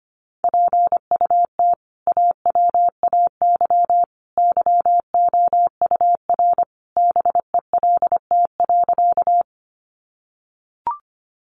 Morse code, heard as a keyed tone, about 25 wpm, 700 Hz. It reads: PUT AWAY YOUR BELT.
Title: Sentences from Top 1400 Words 25wpm